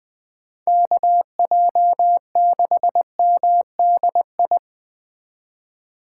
Morse code, heard as a keyed tone, about 20 wpm, 700 Hz. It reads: KJ6MDI
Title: USA Call Signs - 6 Characters - 4x Character Spacing - 20wpm